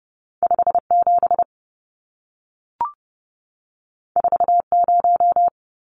Morse code, heard as a keyed tone, about 30 wpm, 700 Hz. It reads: 57 40
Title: Numbers - 2-Digit 30wpm